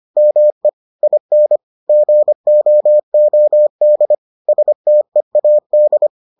Morse code, heard as a keyed tone, about 25 wpm, 600 Hz. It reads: ME IN GOOD STEAD